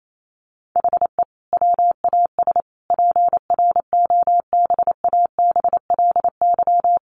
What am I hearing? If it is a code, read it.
Morse code, heard as a keyed tone, about 28 wpm, 700 Hz. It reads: HE WAS PROBABLY
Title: Sentences from Top 1300 Words 28wpm